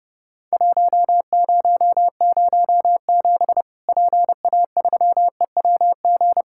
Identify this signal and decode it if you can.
Morse code, heard as a keyed tone, about 30 wpm, 700 Hz. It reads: 1007 PA3EWG
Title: Worked All Europe DX Contest - QTC Exchange - 30wpm